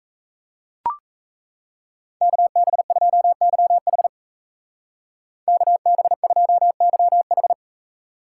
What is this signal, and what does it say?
Morse code, heard as a keyed tone, about 35 wpm, 700 Hz. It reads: KB2YH KB2YH